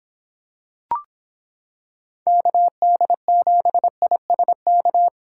Morse code, heard as a keyed tone, about 26 wpm, 700 Hz. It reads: KD7ISK